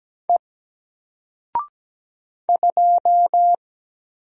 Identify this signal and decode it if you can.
Morse code, heard as a keyed tone, about 17 wpm, 700 Hz. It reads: E 2
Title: Single Character (Letter:Number:Punctuation:Prosign) 17wpm